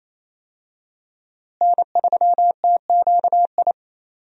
Morse code, heard as a keyed tone, about 28 wpm, 700 Hz. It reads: N3TQI
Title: USA Call Signs - 5 Characters - Repeated 2x - 28wpm